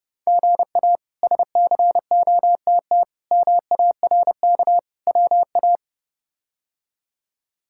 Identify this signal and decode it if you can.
Morse code, heard as a keyed tone, about 32 wpm, 700 Hz. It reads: GA SCOTT MARK WA
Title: K1USN SST QSO Practice - 30wpm